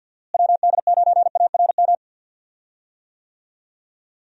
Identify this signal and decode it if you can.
Morse code, heard as a keyed tone, about 50 wpm, 700 Hz. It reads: WD9ARM
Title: USA Call Signs - 6 Characters - Repeated 2x - 50wpm